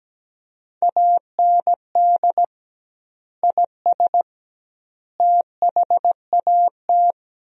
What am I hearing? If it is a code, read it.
Morse code, heard as a keyed tone, about 17 wpm, 700 Hz. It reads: AND IS THAT
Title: 2x Word-Spacing - Sentences from Top 100 Words - 17wpm